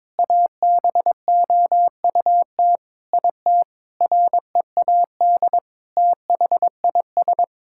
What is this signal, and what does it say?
Morse code, heard as a keyed tone, about 22 wpm, 700 Hz. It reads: ABOUT IT READ THIS
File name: Sentences with 7-Word Phrases - 22wpm